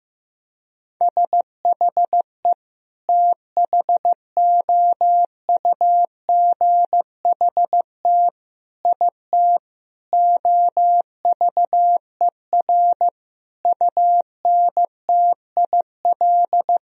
Morse code, heard as a keyed tone, about 15 wpm, 700 Hz. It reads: SHE THOUGHT IT OVER UNTIL
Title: Sentences from Top 1000 Words - Review Part 1 - 15wpm